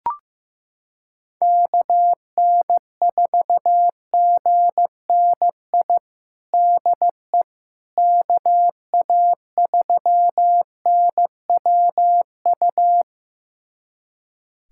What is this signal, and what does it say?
Morse code, heard as a keyed tone, about 15 wpm, 700 Hz. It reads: KN4GNI DE KA3NWU
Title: USA Callsign DE Callsign - 6 Characters - 15wpm